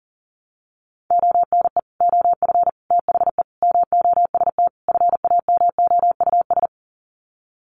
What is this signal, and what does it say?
Morse code, heard as a keyed tone, about 40 wpm, 700 Hz. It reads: ONE OF THE MOST FAMOUS
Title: Sets of 5 Words Encore2 40wpm